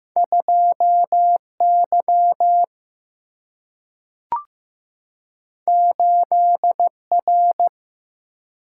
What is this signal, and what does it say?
Morse code, heard as a keyed tone, about 15 wpm, 700 Hz. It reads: 2Y 8R
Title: Single-Letter Single-Number 15wpm